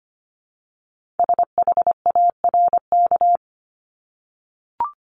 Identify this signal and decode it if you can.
Morse code, heard as a keyed tone, about 25 wpm, 700 Hz. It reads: SHARK
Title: Multi-letter Phonemes Repeated 3x 25wpm